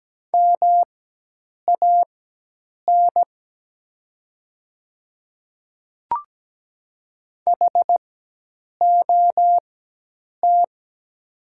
Morse code, heard as a keyed tone, about 17 wpm, spaced as for 8 wpm, 690 Hz. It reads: MAN HOT
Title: 4x Character Spacing - Top 200 Words - 17wpm